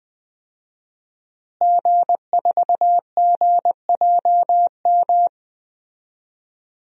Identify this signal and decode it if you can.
Morse code, heard as a keyed tone, about 20 wpm, 700 Hz. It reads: G4GJM
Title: DXCC-100 Call Signs - 5 Characters - Repeated 2x - 20wpm